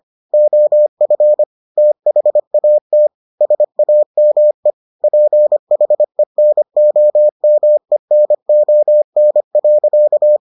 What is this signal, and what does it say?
Morse code, heard as a keyed tone, about 25 wpm, 600 Hz. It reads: OF THAT SAME PHENOMENON.